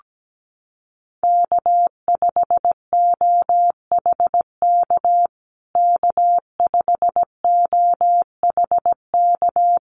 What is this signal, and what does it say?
Morse code, heard as a keyed tone, about 17 wpm, 700 Hz. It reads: K5OHK K5OHK
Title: USA Call Signs Repeated 2x 17wpm